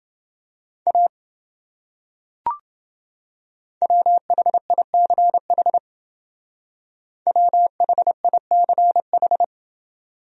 Morse code, heard as a keyed tone, about 28 wpm, 700 Hz. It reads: A WHICH WHICH